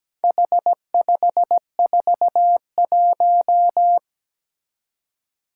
Morse code, heard as a keyed tone, about 17 wpm, 700 Hz. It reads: H541
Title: Four Challenging Letter-Number Combinations 17wpm